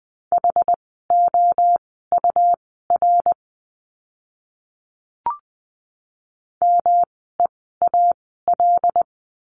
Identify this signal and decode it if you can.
Morse code, heard as a keyed tone, about 20 wpm, spaced as for 15 wpm, 700 Hz. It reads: HOUR MEAL
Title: 108 Lesson 18f - Letter m - Words 20wpm